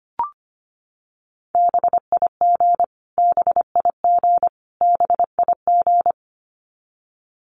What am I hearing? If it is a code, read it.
Morse code, heard as a keyed tone, about 25 wpm, 700 Hz. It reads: BIG BIG BIG
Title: Top 500 Words Repeated 25wpm